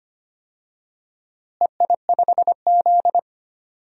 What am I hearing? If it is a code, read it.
Morse code, heard as a keyed tone, about 25 wpm, 700 Hz. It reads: EI5Z
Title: DXCC-100 Call Signs - 4 Characters - Repeated 3x - 25wpm